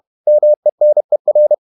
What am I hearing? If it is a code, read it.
Morse code, heard as a keyed tone, about 31 wpm, 600 Hz. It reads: MENER
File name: Ferie QRQ 31wpm